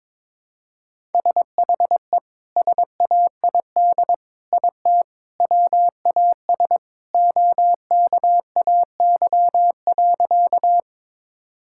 Morse code, heard as a keyed tone, about 22 wpm, 700 Hz. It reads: SHE SAID IT WAS OKAY.
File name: practice / 22wpm / Sentences with 2-Word Phrases - 22wpm